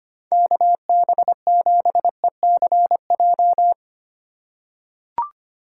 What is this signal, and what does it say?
Morse code, heard as a keyed tone, about 25 wpm, 700 Hz. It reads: KB7ECJ
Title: USA Call Signs - 6 Characters - 3x Character Spacing - 25wpm